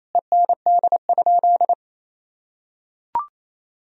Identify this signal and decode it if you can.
Morse code, heard as a keyed tone, about 28 wpm, 700 Hz. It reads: END?
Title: Sentences from Top 1000 Words - Review Part 3 - 28wpm